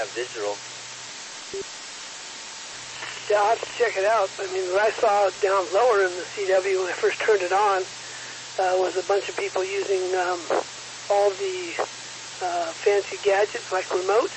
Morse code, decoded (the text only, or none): none